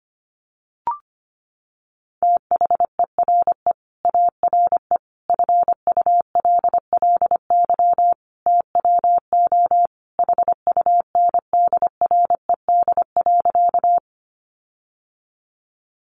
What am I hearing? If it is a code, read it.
Morse code, heard as a keyed tone, about 25 wpm, 700 Hz. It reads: THERE ARE FULLY TWO HUNDRED.
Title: Sentences from Top 1300 Words 25wpm